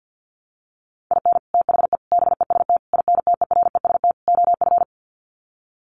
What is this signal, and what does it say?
Morse code, heard as a keyed tone, about 50 wpm, 700 Hz. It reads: IN THE BEST INTEREST OF
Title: Sets of 5 Words Encore 50wpm